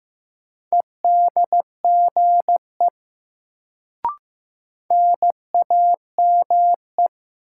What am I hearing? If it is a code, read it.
Morse code, heard as a keyed tone, about 15 wpm, 700 Hz. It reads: EDGE NAME
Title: Sets of 3 Words - New Word Drill - 15wpm